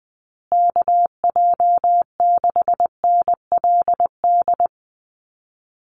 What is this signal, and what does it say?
Morse code, heard as a keyed tone, about 20 wpm, 700 Hz. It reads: KJ6NLD
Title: USA Call Signs Repeated 2x 20wpm